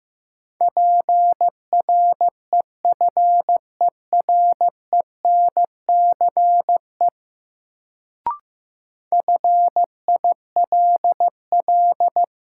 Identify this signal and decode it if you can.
Morse code, heard as a keyed tone, about 15 wpm, 700 Hz. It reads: PREFERENCE FILL
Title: Sentences from Top 1500 Words - New Word Drill Alt - 15wpm